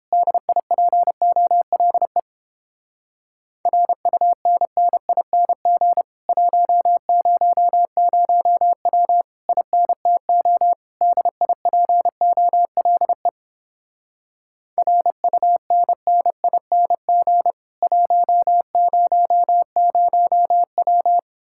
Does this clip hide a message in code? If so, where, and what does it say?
Yes: Morse code from 0.1 to 21.2 s, heard as a keyed tone, about 29 wpm, 700 Hz. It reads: DIPOLE RUNNING 100W INTO DIPOLE RUNNING 100W